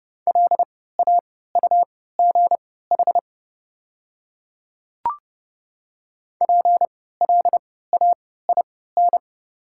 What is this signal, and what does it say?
Morse code, heard as a keyed tone, about 30 wpm, spaced as for 17 wpm, 700 Hz. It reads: LAUGH PLAIN
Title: 3x Character Spacing - Top 500 Words - 30wpm